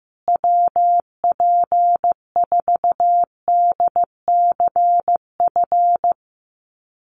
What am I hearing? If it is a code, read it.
Morse code, heard as a keyed tone, about 15 wpm, 700 Hz. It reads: WP4DCF
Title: USA Call Signs Repeated 2x 15wpm